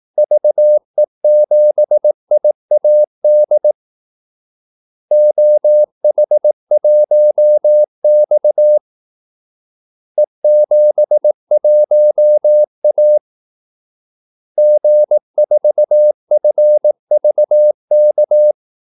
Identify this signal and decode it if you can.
Morse code, heard as a keyed tone, about 18 wpm, 600 Hz. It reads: VE7IAD OH1X E71A G4FVK
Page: Kaldesignaler 18 wpm